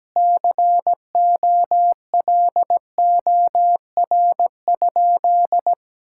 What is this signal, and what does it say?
Morse code, heard as a keyed tone, about 17 wpm, 700 Hz. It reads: COLOR?
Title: Sentences from Top 1000 Words - Review Part 6 - 17wpm